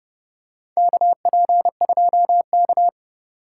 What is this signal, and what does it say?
Morse code, heard as a keyed tone, about 30 wpm, 700 Hz. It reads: KP2K
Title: USA Call Signs - 4 Characters - 3x Character Spacing - 30wpm